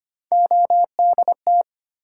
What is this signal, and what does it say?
Morse code, heard as a keyed tone, about 25 wpm, 700 Hz. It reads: ODT
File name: Three Letter-Number Combinations - 25wpm